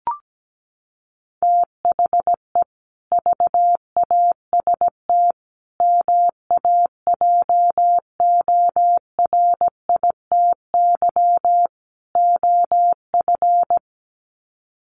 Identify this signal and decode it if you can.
Morse code, heard as a keyed tone, about 17 wpm, 700 Hz. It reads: THE VAST MAJORITY OF
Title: Sets of 4 Words 17wpm